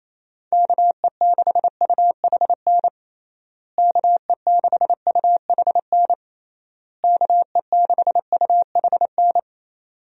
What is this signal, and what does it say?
Morse code, heard as a keyed tone, about 28 wpm, 700 Hz. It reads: KE6UHN KE6UHN KE6UHN
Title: USA Call Signs - 6 Characters - Repeated 3x - 28wpm